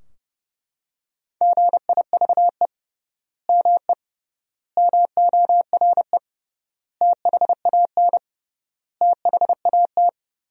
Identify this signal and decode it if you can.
Morse code, heard as a keyed tone, about 30 wpm, 700 Hz. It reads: GIVE ME MORE THAN THAT
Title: 3x Word-Spacing - Sentences from Top 200 Words - 30wpm